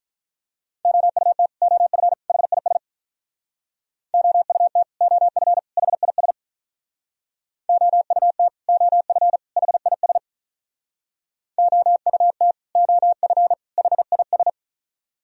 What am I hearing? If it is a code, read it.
Morse code, heard as a keyed tone, about 45 wpm, 700 Hz. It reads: OUT OF HIS OUT OF HIS OUT OF HIS OUT OF HIS